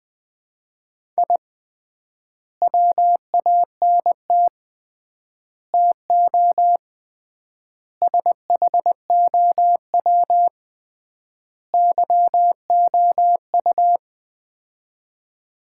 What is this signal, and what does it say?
Morse code, heard as a keyed tone, about 20 wpm, 700 Hz. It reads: I WANT TO SHOW YOU
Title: 3x Word-Spacing - Sets of 5 Words - 20wpm